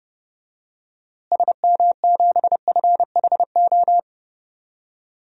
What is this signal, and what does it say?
Morse code, heard as a keyed tone, about 31 wpm, 700 Hz. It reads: SM7FHO
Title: DXCC-100 Call Signs - 6 Characters - Repeated 3x - 30wpm